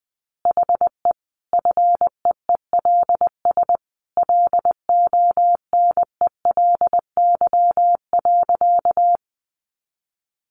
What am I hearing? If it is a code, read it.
Morse code, heard as a keyed tone, about 20 wpm, 700 Hz. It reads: HE FEELS LONELY.
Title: Easy Phrases 20wpm